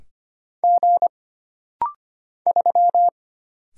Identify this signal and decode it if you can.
Morse code, heard as a keyed tone, about 25 wpm, 700 Hz. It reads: G 3
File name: Single Character (Letter:Number:Punctuation:Prosign) - Rapid-Fire - 25wpm